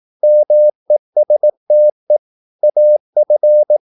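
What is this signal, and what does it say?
Morse code, heard as a keyed tone, about 18 wpm, 600 Hz. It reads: MESTE AF
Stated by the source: Slow (QRS) Morse practice at 18 wpm, CW med Gnister.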